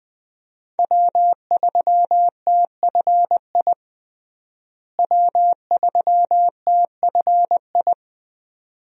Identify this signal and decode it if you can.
Morse code, heard as a keyed tone, about 20 wpm, 700 Hz. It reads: W3TFI W3TFI
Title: USA Call Signs - 5 Characters - Repeated 3x - 20wpm